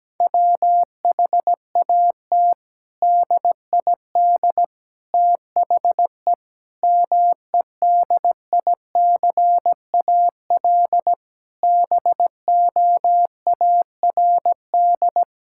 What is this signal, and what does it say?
Morse code, heard as a keyed tone, about 17 wpm, 700 Hz. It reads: WHAT DID THE MEDICAL BOARD
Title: Sentences from Top 1000 Words - Review Part 5 - 17wpm